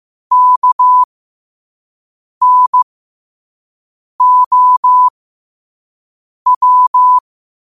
Morse code, heard as a keyed tone, about 15 wpm, spaced as for 5 wpm, 1000 Hz. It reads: KNOW
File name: qotdfiles / 5wpm